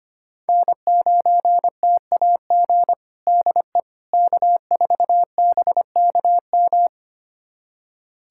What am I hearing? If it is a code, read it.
Morse code, heard as a keyed tone, about 25 wpm, 700 Hz. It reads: N9TAG DE K4BKM
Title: USA Callsign DE Callsign - 5 Characters - 25wpm